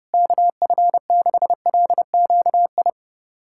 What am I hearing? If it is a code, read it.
Morse code, heard as a keyed tone, about 30 wpm, 700 Hz. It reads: KF6LQI
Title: USA Callsign DE Callsign - 6 Characters - 30wpm